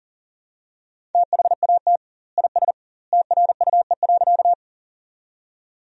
Morse code, heard as a keyed tone, about 40 wpm, 700 Hz. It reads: THAT IS TRUE.
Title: 2x Word-Spacing - Sentences from Top 400 Words - 40wpm